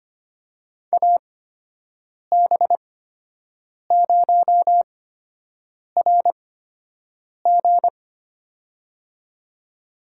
Morse code, heard as a keyed tone, about 25 wpm, spaced as for 7 wpm, 700 Hz. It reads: AB0RG
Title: USA Call Signs - 8x Character Spacing - 25wpm